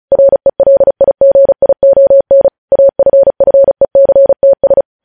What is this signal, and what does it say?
Morse code, heard as a keyed tone, about 36 wpm, 550 Hz. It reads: RELIGION AFFECTS
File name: podcast / News headlines-1x-35x35